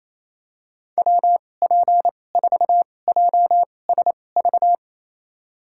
Morse code, heard as a keyed tone, about 27 wpm, spaced as for 20 wpm, 700 Hz. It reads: WP4JSV
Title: USA Call Signs - 6 Characters - 2x Character Spacing - 28wpm